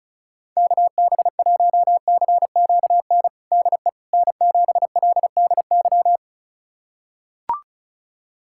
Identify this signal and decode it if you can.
Morse code, heard as a keyed tone, about 34 wpm, 700 Hz. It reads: KB1CQN DE N7LDY
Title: USA Callsign DE Callsign 35wpm